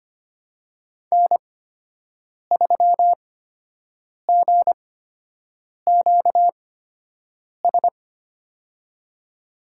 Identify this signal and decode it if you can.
Morse code, heard as a keyed tone, about 25 wpm, spaced as for 7 wpm, 700 Hz. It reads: N3GQS
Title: USA Call Signs - 8x Character Spacing - 25wpm